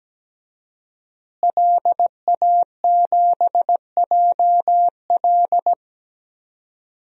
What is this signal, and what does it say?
Morse code, heard as a keyed tone, about 17 wpm, 700 Hz. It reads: LA7JL
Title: DXCC-100 Call Signs - 5 Characters - Repeated 3x - 17wpm